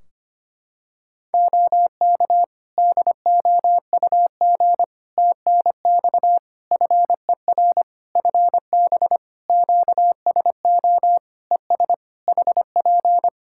Morse code, heard as a keyed tone, about 25 wpm, 700 Hz. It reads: OK DOUG TNX FER FB QSO ES HP
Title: Practice QSO - Exchange 4 - Ending - 25wpm